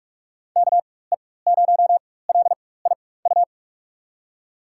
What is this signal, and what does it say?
Morse code, heard as a keyed tone, about 45 wpm, spaced as for 21 wpm, 700 Hz. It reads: KE0LIU
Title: USA Call Signs - 6 Characters - 4x Character Spacing - 45wpm